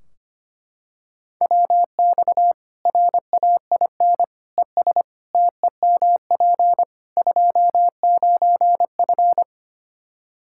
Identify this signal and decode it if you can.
Morse code, heard as a keyed tone, about 25 wpm, 700 Hz. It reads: WX RAIN ES TEMP 29F
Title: Practice QSO - Exchange 2 - Info - 25wpm